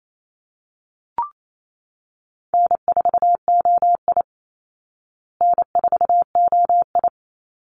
Morse code, heard as a keyed tone, about 28 wpm, 700 Hz. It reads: N4OI N4OI
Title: Easy Call Signs Repeated 3x - 28wpm